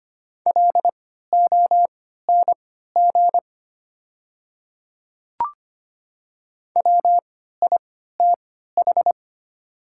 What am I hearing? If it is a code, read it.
Morse code, heard as a keyed tone, about 25 wpm, spaced as for 14 wpm, 700 Hz. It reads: LONG WITH
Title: 3x Character Spacing - Top 100 Words - 25wpm